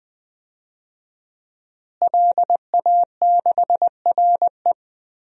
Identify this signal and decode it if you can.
Morse code, heard as a keyed tone, about 20 wpm, 700 Hz. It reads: LA6RE
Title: DXCC-100 Call Signs - 5 Characters - Repeated 3x - 20wpm